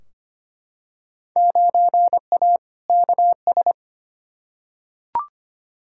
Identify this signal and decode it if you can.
Morse code, heard as a keyed tone, about 25 wpm, 700 Hz. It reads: 9A KS
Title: Field Day Exchanges Encore - 25wpm